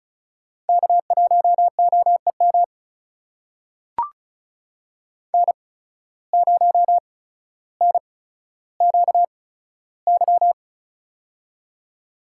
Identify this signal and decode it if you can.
Morse code, heard as a keyed tone, about 36 wpm, spaced as for 15 wpm, 700 Hz. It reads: K1OEM N0NQY
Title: USA Call Signs - 5 Characters - 8x Character Spacing - 35wpm